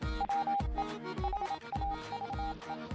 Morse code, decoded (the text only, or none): WE HAVE